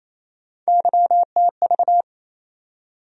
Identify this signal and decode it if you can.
Morse code, heard as a keyed tone, about 28 wpm, 700 Hz. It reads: YTV
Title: USA Call Sign Postfixes - 3 Letters 28wpm